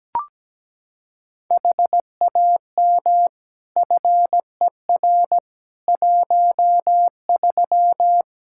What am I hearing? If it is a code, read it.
Morse code, heard as a keyed tone, about 17 wpm, 700 Hz. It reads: HAM FER 13
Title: Practice QSO - Exchange 3 - More Info - 17wpm